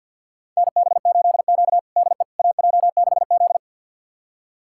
Morse code, heard as a keyed tone, about 46 wpm, 700 Hz. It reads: NB8Q DE AJ6Z
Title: USA Callsign DE Callsign - 4 Characters - 50wpm